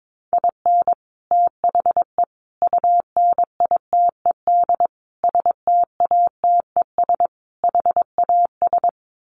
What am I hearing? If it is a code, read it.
Morse code, heard as a keyed tone, about 22 wpm, 700 Hz. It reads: IN THE UNITED STATES HAS
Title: Sets of 5 Words Encore2 22wpm